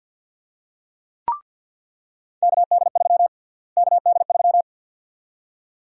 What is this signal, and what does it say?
Morse code, heard as a keyed tone, about 48 wpm, 700 Hz. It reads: KD3 KD3
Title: USA Call Sign Prefixes - Repeated 2x - 50wpm